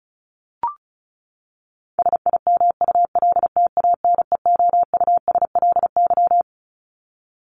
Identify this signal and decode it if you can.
Morse code, heard as a keyed tone, about 34 wpm, 700 Hz. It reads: SIMULTANEOUSLY
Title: LY Suffix - Top 100 Words with Base Words 35wpm